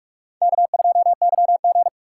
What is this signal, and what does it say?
Morse code, heard as a keyed tone, about 62 wpm, spaced as for 53 wpm, 700 Hz. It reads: K2YG
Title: Speed Racing - USA 1x2 Call Signs 45wpm